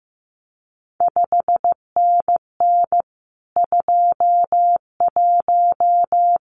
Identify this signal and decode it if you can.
Morse code, heard as a keyed tone, about 15 wpm, 700 Hz. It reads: 5NN 21
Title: CQ WW DX Contest Exchange Encore - 15wpm